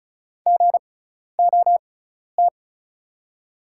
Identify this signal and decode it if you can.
Morse code, heard as a keyed tone, about 34 wpm, spaced as for 12 wpm, 700 Hz. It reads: GOT
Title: 6x Character Spacing - Top 300 Words - 35wpm